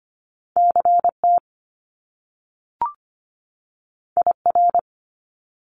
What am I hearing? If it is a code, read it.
Morse code, heard as a keyed tone, about 25 wpm, 700 Hz. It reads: CT IR
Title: Top Two Letter Combinations in Words 25wpm